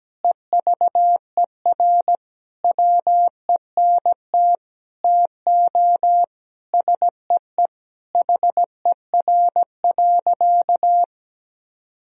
Morse code, heard as a keyed tone, about 17 wpm, 700 Hz. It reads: EVER WENT TO SEE HER.
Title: Sentences from Top 1000 Words - Review Part 1 - 17wpm